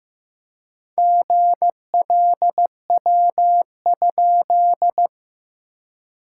Morse code, heard as a keyed tone, about 15 wpm, 700 Hz. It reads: GLW?
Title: SOTA QSO Practice 15wpm